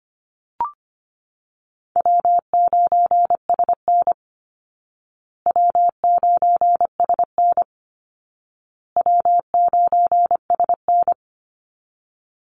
Morse code, heard as a keyed tone, about 25 wpm, 700 Hz. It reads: W9SN W9SN W9SN
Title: Easy Call Signs Repeated 3x - 25wpm